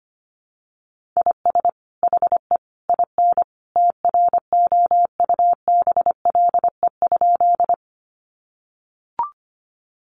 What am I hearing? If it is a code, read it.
Morse code, heard as a keyed tone, about 25 wpm, 700 Hz. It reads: IS HE IN TROUBLE?
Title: Sentences from Top 1200 Words 25wpm